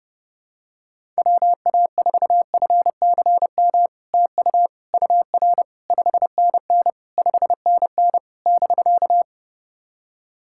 Morse code, heard as a keyed tone, about 30 wpm, 700 Hz. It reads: WA4FCM TU UR 5NN 5NN <BK>
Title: SOTA QSO Practice 30wpm